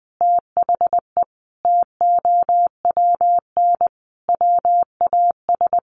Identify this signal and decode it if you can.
Morse code, heard as a keyed tone, about 20 wpm, 700 Hz. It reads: THE TOWN WAS
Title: Sentences from Top 1500 Words 20wpm